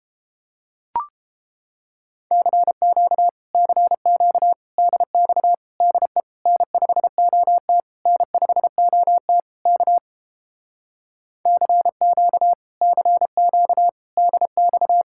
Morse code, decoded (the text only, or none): CQ CQ DX DE N5OT N5OT K CQ CQ DX